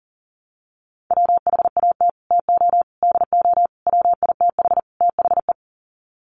Morse code, heard as a keyed tone, about 40 wpm, 700 Hz. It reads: WHAT TO DO WITH THE
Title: Sets of 5 Words Encore2 40wpm